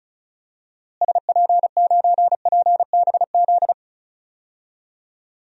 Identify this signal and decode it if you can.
Morse code, heard as a keyed tone, about 36 wpm, 700 Hz. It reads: SP9PBZ
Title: DXCC-100 Call Signs - 6 Characters - Repeated 2x - 35wpm